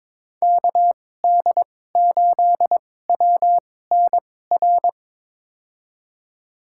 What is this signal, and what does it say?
Morse code, heard as a keyed tone, about 22 wpm, spaced as for 16 wpm, 700 Hz. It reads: KD8WNR
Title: USA Call Signs - 6 Characters - 2x Character Spacing - 22wpm